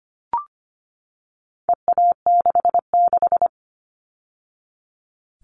Morse code, heard as a keyed tone, about 25 wpm, 700 Hz. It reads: EA66
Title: All Cut Numbers - 4-Digit 25wpm